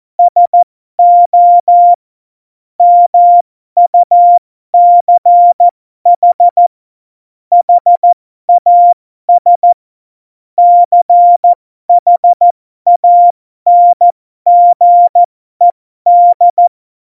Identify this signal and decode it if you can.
Morse code, heard as a keyed tone, about 14 wpm, spaced as for 12 wpm, 700 Hz. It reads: SO MUCH HAS CHANGED
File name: Morsekurs / Witze